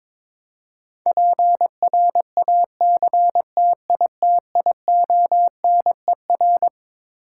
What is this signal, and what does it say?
Morse code, heard as a keyed tone, about 22 wpm, 700 Hz. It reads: PRACTITIONER
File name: Twelve Letter Words - 22wpm